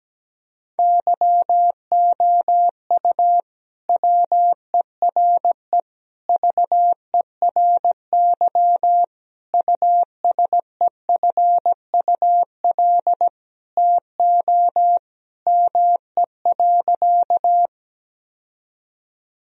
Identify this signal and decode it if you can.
Morse code, heard as a keyed tone, about 17 wpm, 700 Hz. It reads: YOU WERE VERY USEFUL TO ME.
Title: Sentences from Top 1000 Words - Review Part 4 - 17wpm